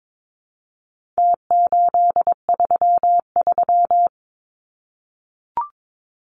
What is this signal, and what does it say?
Morse code, heard as a keyed tone, about 22 wpm, 700 Hz. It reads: T833
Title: Numbers - Contest Style - 4-Digit 22wpm